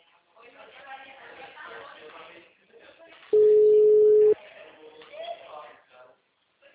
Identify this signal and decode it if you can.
Ringback tone (Europe, 425 Hz), with speech.